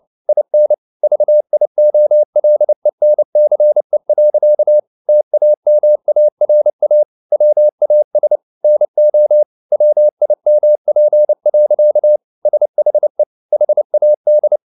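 Morse code, heard as a keyed tone, about 29 wpm, 600 Hz. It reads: IN VIOLENCE. TAMARA WAS NO WIMP. SHE HAD
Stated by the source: Morse code at 29 wpm, sending Ken Follett's novel Never in English, from CW med Gnister.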